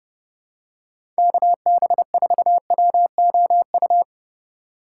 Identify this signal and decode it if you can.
Morse code, heard as a keyed tone, about 29 wpm, 700 Hz. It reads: KB4WOU
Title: USA Call Signs - 6 Characters - 4x Character Spacing - 30wpm